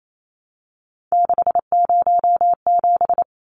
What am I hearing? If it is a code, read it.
Morse code, heard as a keyed tone, about 28 wpm, 700 Hz. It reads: 607
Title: Numbers - 3-Digit 28wpm